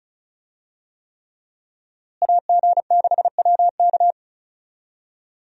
Morse code, heard as a keyed tone, about 34 wpm, 700 Hz. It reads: AG6WK
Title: USA Call Signs - 5 Characters - Repeated 3x - 35wpm